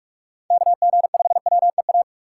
Morse code, heard as a keyed tone, about 46 wpm, 700 Hz. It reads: KG5WEA
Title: USA Call Signs - 6 Characters - 3x Character Spacing - 45wpm